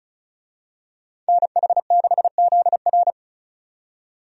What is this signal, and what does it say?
Morse code, heard as a keyed tone, about 36 wpm, 700 Hz. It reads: NH6ZR
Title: USA Call Signs - 5 Characters - 35wpm